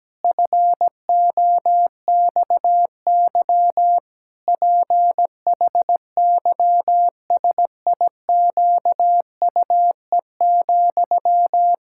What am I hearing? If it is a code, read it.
Morse code, heard as a keyed tone, about 17 wpm, 700 Hz. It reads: FOXY PHYSIQUE,
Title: Pangram Challenge v2 - 17wpm